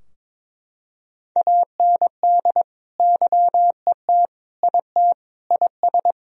Morse code, heard as a keyed tone, about 22 wpm, 690 Hz. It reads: AND YET IT IS
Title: Sentences from Top 1000 Words - Review Part 3 - 22wpm